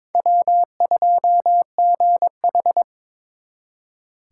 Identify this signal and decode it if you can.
Morse code, heard as a keyed tone, about 25 wpm, 700 Hz. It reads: W2GH
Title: Speed Racing - USA 1x2 Call Signs 22wpm